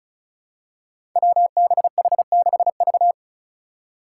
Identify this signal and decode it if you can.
Morse code, heard as a keyed tone, about 36 wpm, 700 Hz. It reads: WBH6V
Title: Five Challenging Letter-Number Combinations 35wpm